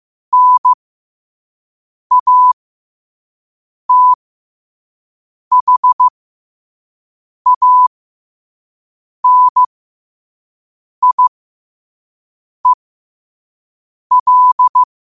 Morse code, heard as a keyed tone, about 15 wpm, spaced as for 5 wpm, 1000 Hz. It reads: NATHANIEL
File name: qotdfiles / 5wpm